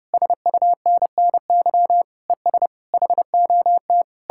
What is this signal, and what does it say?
Morse code, heard as a keyed tone, about 38 wpm, 700 Hz. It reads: SUNNY ES HOT